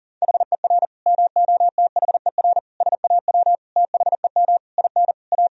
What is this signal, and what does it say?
Morse code, heard as a keyed tone, about 40 wpm, 700 Hz. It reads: HER MOTHER SAW THEM IN A
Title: Sentences from Top 1000 Words - Review Part 3 - 40wpm